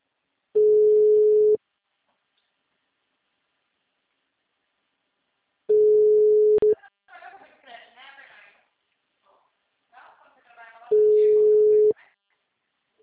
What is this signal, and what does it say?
Ringback tone (Europe, 425 Hz), with speech.
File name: out- -2012- - - .19020